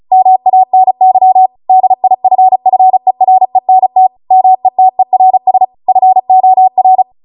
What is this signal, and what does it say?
Morse code, heard as a keyed tone, about 36 wpm, 750 Hz. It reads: MANY DIFFERENT METERS FOR